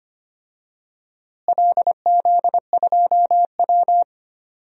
Morse code, heard as a keyed tone, about 25 wpm, 700 Hz. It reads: LZ2W
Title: DXCC-100 Call Signs - 4 Characters - Repeated 2x - 25wpm